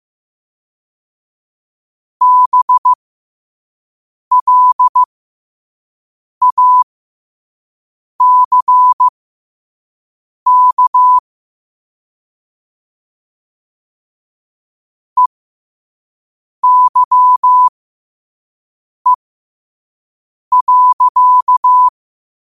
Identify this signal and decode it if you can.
Morse code, heard as a keyed tone, about 15 wpm, spaced as for 5 wpm, 1000 Hz. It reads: BLACK EYE.
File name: qotdfiles / 5wpm